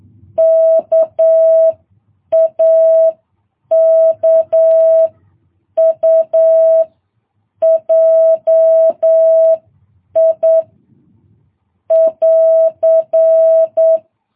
Morse code, heard as a keyed tone, about 9 wpm, spaced as for 7 wpm, 650 Hz. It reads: KAKUJI <AR>
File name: jh8cbh